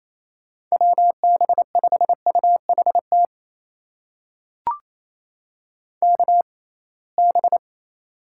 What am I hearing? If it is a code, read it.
Morse code, heard as a keyed tone, about 28 wpm, spaced as for 21 wpm, 700 Hz. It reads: WB5UHT KB
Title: USA Call Signs - 6 Characters - 6x Character Spacing - 28wpm